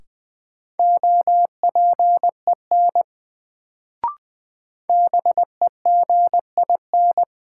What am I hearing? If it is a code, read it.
Morse code, heard as a keyed tone, about 20 wpm, 700 Hz. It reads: OPEN BEGIN
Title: Sentences from Top 300 Words - New Word Drill - 20wpm